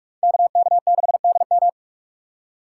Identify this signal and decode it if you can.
Morse code, heard as a keyed tone, about 45 wpm, 700 Hz. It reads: KK6DM
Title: USA Call Signs - 5 Characters - 4x Character Spacing - 45wpm